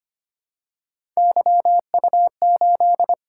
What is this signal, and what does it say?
Morse code, heard as a keyed tone, about 25 wpm, 700 Hz. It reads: YU8
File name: Three Letter-Number Combinations - 25wpm